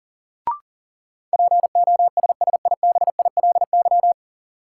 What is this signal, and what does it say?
Morse code, heard as a keyed tone, about 40 wpm, 700 Hz. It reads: POSSIBILY
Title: Sentences from Top 1300 Words - New Word Drill - 40wpm